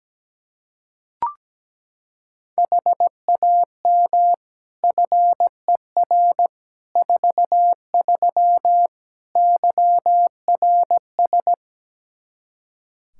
Morse code, heard as a keyed tone, about 17 wpm, 700 Hz. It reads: HAM FER 43 YRS
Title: K1USN SST QSO Practice 17wpm